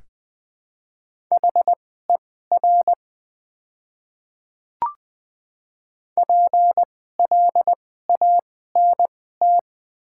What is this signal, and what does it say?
Morse code, heard as a keyed tone, about 20 wpm, spaced as for 15 wpm, 700 Hz. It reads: HER PLANT
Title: 2x Character Spacing - Top 500 Words - 20wpm